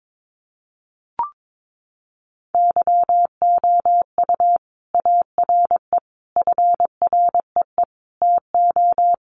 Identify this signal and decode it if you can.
Morse code, heard as a keyed tone, about 22 wpm, 700 Hz. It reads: YOU ARE FREE TO
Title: Sentences from Top 700 Words 22wpm